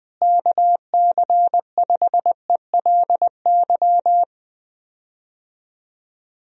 Morse code, heard as a keyed tone, about 20 wpm, 700 Hz. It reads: KC5ELY
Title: USA Call Signs - 6 Characters - Repeated 2x - 20wpm